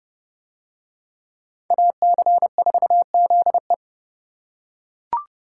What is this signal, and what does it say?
Morse code, heard as a keyed tone, about 29 wpm, 700 Hz. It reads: AC4ZE